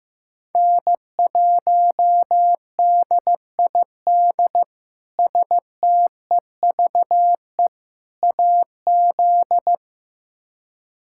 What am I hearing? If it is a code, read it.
Morse code, heard as a keyed tone, about 15 wpm, 700 Hz. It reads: N1DID STEVE AZ
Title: K1USN SST QSO Practice - 15wpm